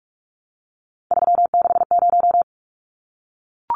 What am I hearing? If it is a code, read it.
Morse code, heard as a keyed tone, about 44 wpm, 700 Hz. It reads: 360
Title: Numbers - 3-Digit 45wpm